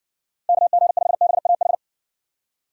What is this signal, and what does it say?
Morse code, heard as a keyed tone, about 60 wpm, 700 Hz. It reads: KG5BAH